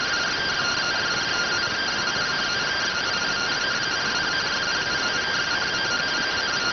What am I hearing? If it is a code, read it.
Two-tone teleprinter signal, about 170 Hz shift.